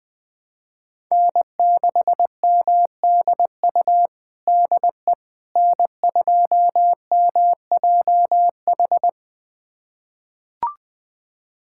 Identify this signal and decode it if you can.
Morse code, heard as a keyed tone, about 20 wpm, 700 Hz. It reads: N6MDU DE N2MJH
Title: USA Callsign DE Callsign - 5 Characters - 20wpm